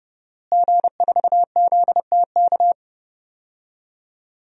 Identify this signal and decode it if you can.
Morse code, heard as a keyed tone, about 30 wpm, 700 Hz. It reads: G4ZTK
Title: DXCC-100 Call Signs - 5 Characters - Repeated 2x - 30wpm